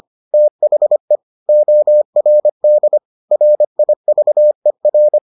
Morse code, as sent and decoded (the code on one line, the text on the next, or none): - .... / --- .-. -.. / .-. .. ...- . .-.
THE ORD RIVER